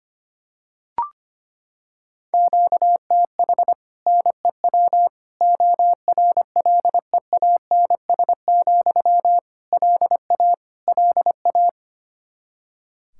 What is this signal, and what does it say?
Morse code, heard as a keyed tone, about 25 wpm, 700 Hz. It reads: QTH NEW ORLEANS, LA LA
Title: Practice QSO - Exchange 1 - Beginning - 25wpm